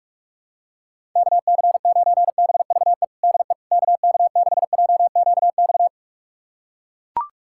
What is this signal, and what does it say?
Morse code, heard as a keyed tone, about 41 wpm, 700 Hz. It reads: KC9BVE DE KK6JQX
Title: USA Callsign DE Callsign - 6 Characters - 45wpm